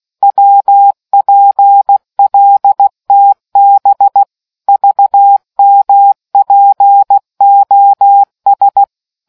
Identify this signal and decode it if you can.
Morse code, heard as a keyed tone, about 16 wpm, 800 Hz. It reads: WPLTB VMPOS